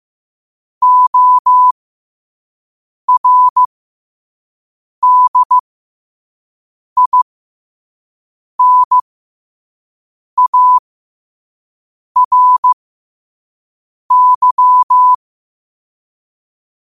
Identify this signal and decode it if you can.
Morse code, heard as a keyed tone, about 15 wpm, spaced as for 5 wpm, 1000 Hz. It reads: ORDINARY
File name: qotdfiles / 5wpm